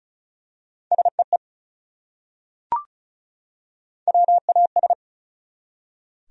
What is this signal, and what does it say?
Morse code, heard as a keyed tone, about 34 wpm, 700 Hz. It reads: SEE WAS
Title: Top 100 Words 35wpm